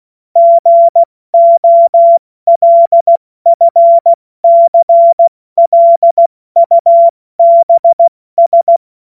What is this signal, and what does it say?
Morse code, heard as a keyed tone, about 16 wpm, 680 Hz. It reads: GOLFCLUBS
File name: Morsekurs / Witze / Witz 09. August 2022 70 F80